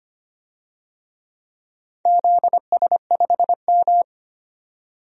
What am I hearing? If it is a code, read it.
Morse code, heard as a keyed tone, about 25 wpm, 700 Hz. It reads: ZS5M
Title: DXCC-100 Call Signs - 4 Characters - 25wpm